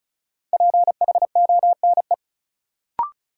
Morse code, heard as a keyed tone, about 36 wpm, 700 Hz. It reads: PHONE
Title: Sentences from Top 1100 Words - New Word Drill Alt - 35wpm